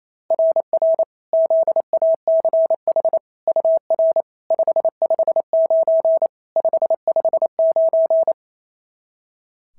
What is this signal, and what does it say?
Morse code, heard as a keyed tone, about 27 wpm, 690 Hz. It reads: RR ZACH UR 559 559
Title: SOTA QSO Practice 28wpm